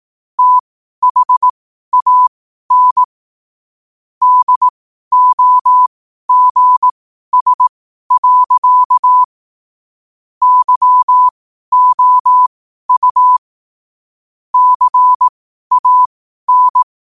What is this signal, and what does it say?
Morse code, heard as a keyed tone, about 18 wpm, spaced as for 12 wpm, 1000 Hz. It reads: THAN DOGS. YOU CAN
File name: qotdfiles / 12wpm